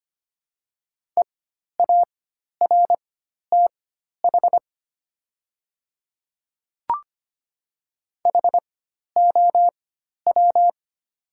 Morse code, heard as a keyed tone, about 25 wpm, spaced as for 12 wpm, 700 Hz. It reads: EARTH HOW
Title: 4x Character Spacing - Top 200 Words - 25wpm